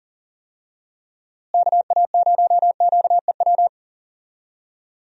Morse code, heard as a keyed tone, about 40 wpm, 700 Hz. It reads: KA0QEW
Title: USA Call Signs - 6 Characters - Repeated 2x - 40wpm